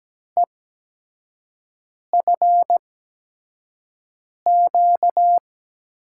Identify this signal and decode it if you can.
Morse code, heard as a keyed tone, about 17 wpm, spaced as for 5 wpm, 700 Hz. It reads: EFQ
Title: USA Call Signs - 6 Characters - 8x Character Spacing - 17wpm